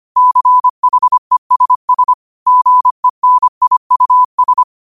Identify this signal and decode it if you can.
Morse code, heard as a keyed tone, about 25 wpm, 1000 Hz. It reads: CHESS GENIUS
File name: qotdfiles / 25wpm